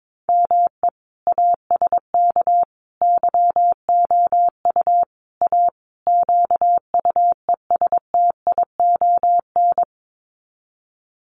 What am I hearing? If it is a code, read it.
Morse code, heard as a keyed tone, about 22 wpm, 700 Hz. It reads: ME ASK YOU A QUESTION
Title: Sets of 5 Words Encore2 22wpm